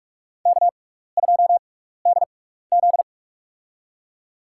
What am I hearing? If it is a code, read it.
Morse code, heard as a keyed tone, about 46 wpm, spaced as for 15 wpm, 700 Hz. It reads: K2DZ
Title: USA Call Signs - 4 Characters - 6x Character Spacing - 45wpm